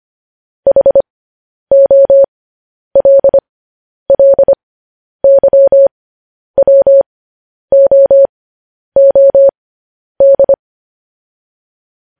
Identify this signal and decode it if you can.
Morse code, heard as a keyed tone, about 25 wpm, spaced as for 10 wpm, 550 Hz. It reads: HOLLYWOOD